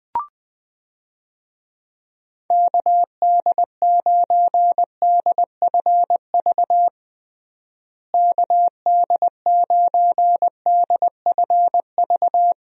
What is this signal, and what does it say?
Morse code, heard as a keyed tone, about 20 wpm, 700 Hz. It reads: KD9DFV KD9DFV
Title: USA Call Signs - 6 Characters - Repeated 3x - 20wpm